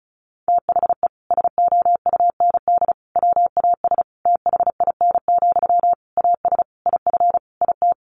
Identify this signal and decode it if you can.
Morse code, heard as a keyed tone, about 34 wpm, 700 Hz. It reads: THE SOUND WAS THIN, AS IF IT
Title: Sentences from Top 1200 Words 35wpm